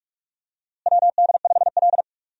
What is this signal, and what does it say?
Morse code, heard as a keyed tone, about 46 wpm, 700 Hz. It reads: WD5L
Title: USA Call Signs - 4 Characters - Repeated 2x - 45wpm